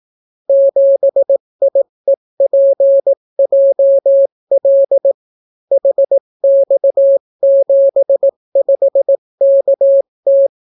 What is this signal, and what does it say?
Morse code, heard as a keyed tone, about 18 wpm, 550 Hz. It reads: 7IEPJL HX75KT